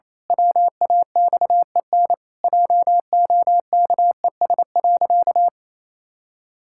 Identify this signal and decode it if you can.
Morse code, heard as a keyed tone, about 28 wpm, 700 Hz. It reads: WAXEN JOKES.
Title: Pangram Challenge v2 - 28wpm